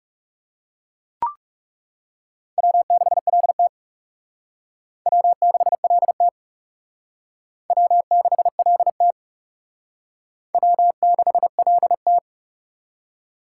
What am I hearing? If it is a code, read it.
Morse code, heard as a keyed tone, about 38 wpm, 700 Hz. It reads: W6LT W6LT W6LT W6LT